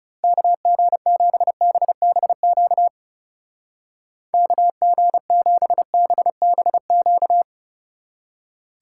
Morse code, heard as a keyed tone, about 35 wpm, 700 Hz. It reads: KG7BBQ KG7BBQ
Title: Speed Racing - USA 2x3 Call Signs 30wpm